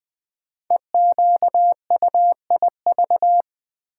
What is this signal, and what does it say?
Morse code, heard as a keyed tone, about 20 wpm, 700 Hz. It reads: EQUIV
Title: Build A Word - 5 Letters At A Time - 10-Letter Words - 20wpm